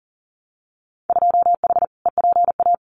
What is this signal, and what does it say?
Morse code, heard as a keyed tone, about 40 wpm, 700 Hz. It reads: 2H EPA
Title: Winter Field Day Exchanges 40wpm